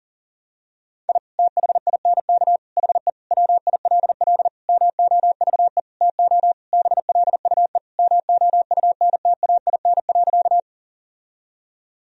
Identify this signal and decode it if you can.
Morse code, heard as a keyed tone, about 40 wpm, 700 Hz. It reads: I THINK HE WILL MOVE TO BLUE MOUNTAIN.
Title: Sentences from Top 1000 Words - Review Part 3 - 40wpm